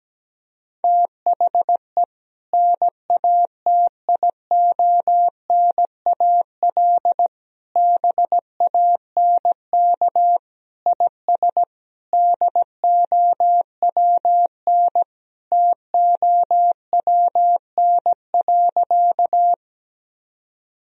Morse code, heard as a keyed tone, about 17 wpm, 700 Hz. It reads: THE NATIONAL BANK IS DOWN TOWN.
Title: Sentences from Top 1000 Words - Review Part 4 - 17wpm